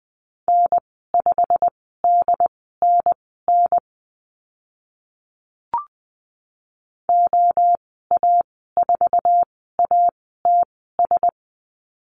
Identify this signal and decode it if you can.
Morse code, heard as a keyed tone, about 20 wpm, spaced as for 14 wpm, 700 Hz. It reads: N5DNN OA4ATS
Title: 088 Lesson 15g - Number 5 - Call Signs 20wpm